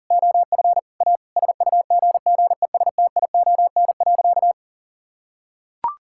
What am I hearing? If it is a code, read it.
Morse code, heard as a keyed tone, about 40 wpm, 700 Hz. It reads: OF A SUGGESTION.
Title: Sentences from Top 1600 Words 40wpm